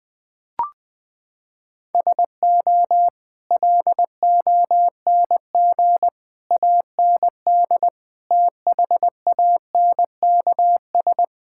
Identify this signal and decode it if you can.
Morse code, heard as a keyed tone, about 21 wpm, 700 Hz. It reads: SO LONG AND THANKS
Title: Sentences with 8-Word Phrases - 20wpm